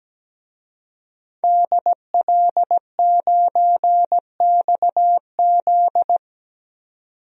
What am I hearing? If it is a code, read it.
Morse code, heard as a keyed tone, about 17 wpm, 700 Hz. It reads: DL9XZ
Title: DXCC-100 Call Signs - 5 Characters - 17wpm